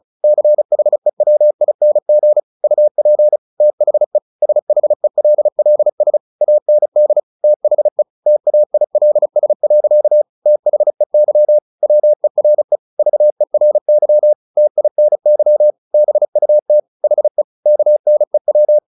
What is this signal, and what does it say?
Morse code, heard as a keyed tone, about 35 wpm, 600 Hz. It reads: CHEWING UP THE SHELLS AND THE TAILS. THEY WERE VERY TINY BUT HE KNEW
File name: the old man and the sea 2 35wpm